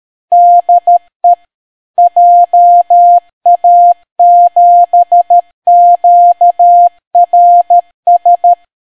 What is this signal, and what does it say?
Morse code, heard as a keyed tone, about 13 wpm, 700 Hz. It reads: DE JA7QRS